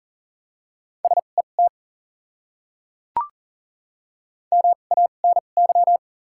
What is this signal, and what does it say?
Morse code, heard as a keyed tone, about 40 wpm, spaced as for 29 wpm, 700 Hz. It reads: SET MANY
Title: 2x Character Spacing - Top 200 Words - 40wpm